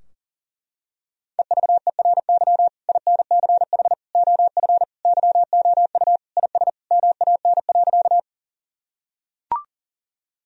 Morse code, heard as a keyed tone, about 40 wpm, 700 Hz. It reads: EVERY INCH OF YOU IS MAN.
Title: Sentences from Top 1000 Words - Review Part 3 - 40wpm